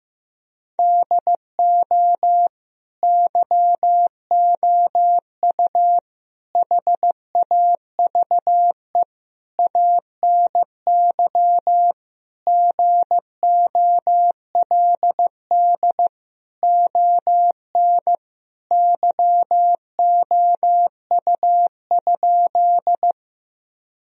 Morse code, heard as a keyed tone, about 15 wpm, 700 Hz. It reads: DO YOU HAVE ANY GOLD ON YOU?
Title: Sentences from Top 1000 Words - Review Part 3 - 15wpm